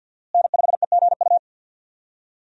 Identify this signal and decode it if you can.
Morse code, heard as a keyed tone, about 50 wpm, 690 Hz. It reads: N5EGU
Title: USA Call Signs - 5 Characters - Repeated 3x - 50wpm